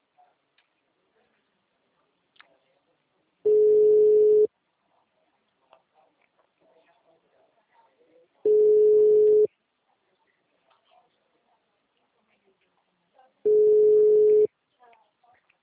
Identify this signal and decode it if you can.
Ringback tone (Europe, 425 Hz), with speech.